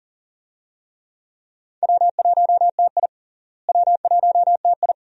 Morse code, heard as a keyed tone, about 38 wpm, 700 Hz. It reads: W1TI W1TI
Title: USA Call Signs - 4 Characters - Repeated 2x - 40wpm